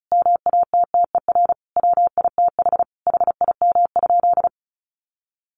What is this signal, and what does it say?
Morse code, heard as a keyed tone, about 34 wpm, 700 Hz. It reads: MATTER WITH HIM?
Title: Sentences from Top 700 Words 35wpm